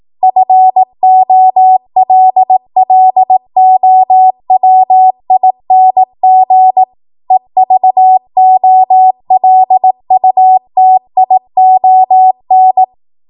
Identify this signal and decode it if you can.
Morse code, heard as a keyed tone, about 18 wpm, 750 Hz. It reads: FOLLOWING EVOLUTION
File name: Archive / 18 WPM / 18WPM